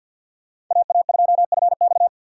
Morse code, heard as a keyed tone, about 53 wpm, 700 Hz. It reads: AA2FX
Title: USA Call Signs - 5 Characters - 4x Character Spacing - 50wpm